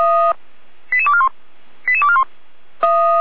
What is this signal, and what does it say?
Five-tone selective call (ZVEI), heard as quick runs of five tones, tones of about 70 ms. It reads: 88241 88241